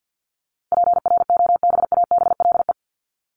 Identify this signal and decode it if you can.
Morse code, heard as a keyed tone, about 50 wpm, 700 Hz. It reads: PROBABLE
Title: LY Suffix - Top 100 Words with Base Words 50wpm